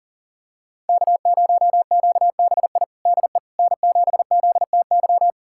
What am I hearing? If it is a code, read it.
Morse code, heard as a keyed tone, about 37 wpm, 700 Hz. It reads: K0QBI DE N7ZTY
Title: USA Callsign DE Callsign - 5 Characters - 40wpm